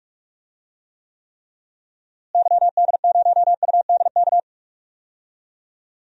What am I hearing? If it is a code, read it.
Morse code, heard as a keyed tone, about 46 wpm, 700 Hz. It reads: YD0UDK
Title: DXCC-100 Call Signs - 6 Characters - Repeated 3x - 45wpm